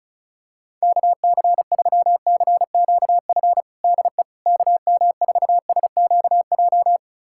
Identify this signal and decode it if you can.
Morse code, heard as a keyed tone, about 36 wpm, 700 Hz. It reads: KC3CQF DE KM4SQJ
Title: USA Callsign DE Callsign - 6 Characters - 35wpm